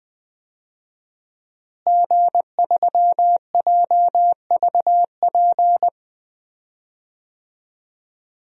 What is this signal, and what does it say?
Morse code, heard as a keyed tone, about 20 wpm, 700 Hz. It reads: G3JVP
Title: DXCC-100 Call Signs - 5 Characters - Repeated 2x - 20wpm